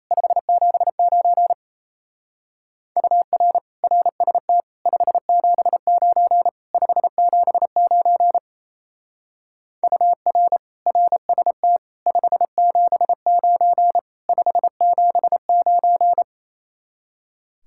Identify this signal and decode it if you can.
Morse code, heard as a keyed tone, about 35 wpm, 700 Hz. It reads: 579 UR RST 579 579 UR RST 579 579